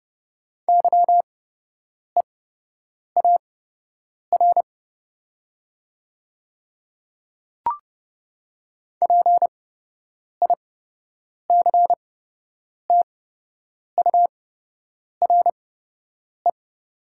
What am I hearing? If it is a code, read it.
Morse code, heard as a keyed tone, about 30 wpm, spaced as for 8 wpm, 700 Hz. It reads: YEAR PICTURE
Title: 8x Character Spacing - Top 200 Words - 30wpm